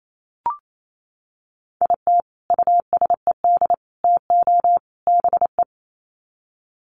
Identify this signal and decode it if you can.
Morse code, heard as a keyed tone, about 28 wpm, 700 Hz. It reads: IT USED TO BE
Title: Sets of 4 Words 28wpm